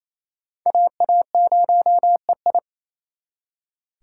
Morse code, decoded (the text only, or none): AA0EI